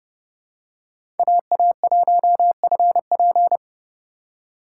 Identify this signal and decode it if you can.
Morse code, heard as a keyed tone, about 30 wpm, 700 Hz. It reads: AA1FP
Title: USA Call Signs - 4x Character Spacing - 30wpm